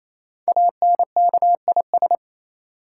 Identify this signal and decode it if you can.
Morse code, heard as a keyed tone, about 29 wpm, 700 Hz. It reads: ANKIS
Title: Code Groups - 5 Letters - 28wpm